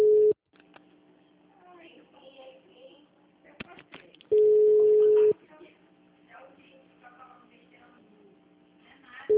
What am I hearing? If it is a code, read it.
Ringback tone (Europe, 425 Hz), with speech.